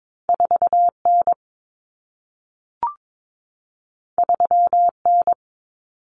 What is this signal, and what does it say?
Morse code, heard as a keyed tone, about 22 wpm, 700 Hz. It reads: 4N 3N
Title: Common Cut Numbers - 2-Digit 22wpm